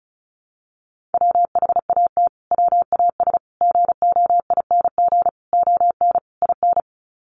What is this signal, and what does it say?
Morse code, heard as a keyed tone, about 34 wpm, 700 Hz. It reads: WHAT WAS GOING ON IN
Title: Sets of 5 Words Encore 35wpm